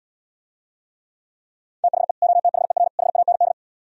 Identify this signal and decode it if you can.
Morse code, heard as a keyed tone, about 77 wpm, 700 Hz. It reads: THE OTHER HAND